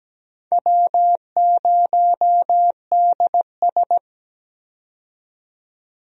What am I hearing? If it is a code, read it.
Morse code, heard as a keyed tone, about 17 wpm, 700 Hz. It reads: W0DS
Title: USA Call Signs - 4 Characters - 17wpm